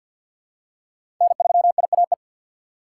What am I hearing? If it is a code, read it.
Morse code, heard as a keyed tone, about 53 wpm, 700 Hz. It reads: N3IAE
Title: USA Call Signs - 2x Character Spacing - 50wpm